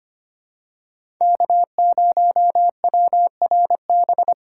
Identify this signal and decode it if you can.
Morse code, heard as a keyed tone, about 25 wpm, 700 Hz. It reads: K0WRB
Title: USA Call Signs - 4x Character Spacing - 25wpm